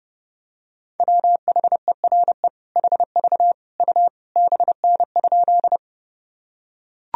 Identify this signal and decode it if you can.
Morse code, heard as a keyed tone, about 30 wpm, 700 Hz. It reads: WHERE HV U BN?
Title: Sentences from Top 200 Words and Abbreviations 30wpm